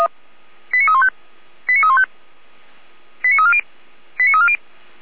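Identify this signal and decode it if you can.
Five-tone selective call (ZVEI), heard as quick runs of five tones, tones of about 70 ms. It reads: 89315 89315 89340 89340
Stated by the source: Five-tone ZVEI selective call received by SDR.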